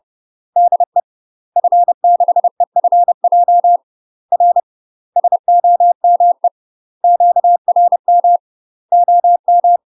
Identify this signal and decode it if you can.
Morse code, heard as a keyed tone, about 30 wpm, 700 Hz. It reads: DE F6EFJ R SOME QRM OM
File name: UK MASTER QSOs TRAINING / 30WPM / UK MASTER QSOs 0041